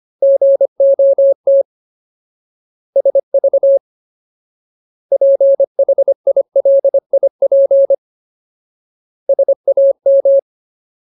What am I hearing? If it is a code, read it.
Morse code, heard as a keyed tone, about 25 wpm, 550 Hz. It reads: GOT SV PHILIP SAM
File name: Ultimate Word List / 25WPM Ultimate